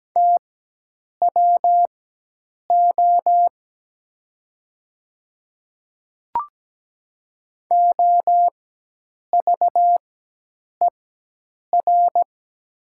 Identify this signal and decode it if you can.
Morse code, heard as a keyed tone, about 17 wpm, spaced as for 8 wpm, 700 Hz. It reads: TWO OVER
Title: 4x Character Spacing - Top 200 Words - 17wpm